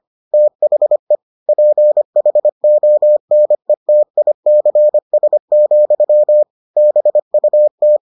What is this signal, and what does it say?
Morse code, heard as a keyed tone, about 25 wpm, 600 Hz. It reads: THE PHONETICS, BUT